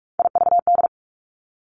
Morse code, heard as a keyed tone, about 46 wpm, 700 Hz. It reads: IVD
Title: Three Challenging Letter-Number Combinations 45wpm